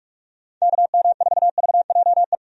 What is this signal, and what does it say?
Morse code, heard as a keyed tone, about 46 wpm, 700 Hz. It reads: KM4VJE
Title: USA Call Signs - 6 Characters - 45wpm